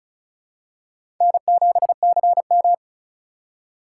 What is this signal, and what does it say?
Morse code, heard as a keyed tone, about 36 wpm, 700 Hz. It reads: N7CM
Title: USA Call Signs - 4 Characters - 4x Character Spacing - 35wpm